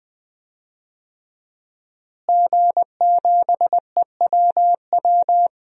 Morse code, heard as a keyed tone, about 20 wpm, 700 Hz. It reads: G7EWW
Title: DXCC-100 Call Signs - 5 Characters - Repeated 2x - 20wpm